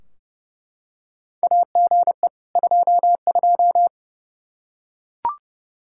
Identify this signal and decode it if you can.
Morse code, heard as a keyed tone, about 28 wpm, 700 Hz. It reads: AGE 22
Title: Practice QSO - Exchange 3 - More Info - 30wpm